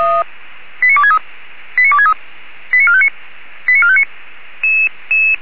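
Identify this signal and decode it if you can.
Five-tone selective call (ZVEI), heard as quick runs of five tones, tones of about 70 ms. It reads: 89263 89263 89459 89459
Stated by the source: Five-tone ZVEI selective call received on an SDR.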